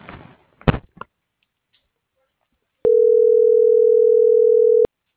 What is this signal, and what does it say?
Ringback tone (North America), 440 + 480 Hz.